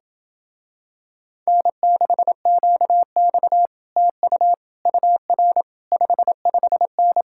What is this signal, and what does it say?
Morse code, heard as a keyed tone, about 28 wpm, 700 Hz. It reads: N6QX TU UR 55N